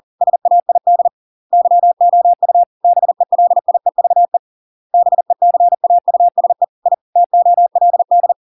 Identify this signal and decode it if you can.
Morse code, heard as a keyed tone, about 40 wpm, 700 Hz. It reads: SAID YOU BELIEVE BECAUSE I TOLD